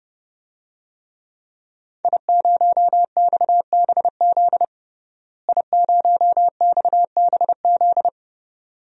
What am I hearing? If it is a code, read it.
Morse code, heard as a keyed tone, about 30 wpm, 700 Hz. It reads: I0XBZ I0XBZ
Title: DXCC-100 Call Signs - 5 Characters - Repeated 2x - 30wpm